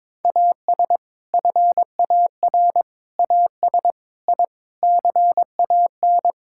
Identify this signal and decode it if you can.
Morse code, heard as a keyed tone, about 22 wpm, 700 Hz. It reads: AS FAR AS I CAN
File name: practice / 22wpm / Sets of 6 Words - 22wpm